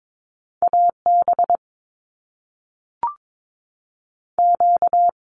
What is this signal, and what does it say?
Morse code, heard as a keyed tone, about 22 wpm, 700 Hz. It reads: AB Q
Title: Canadian Provinces and Territories 22wpm